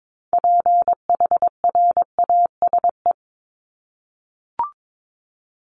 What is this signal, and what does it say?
Morse code, heard as a keyed tone, about 22 wpm, 700 Hz. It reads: PHRASE
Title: Top 1501-1750 Words 22wpm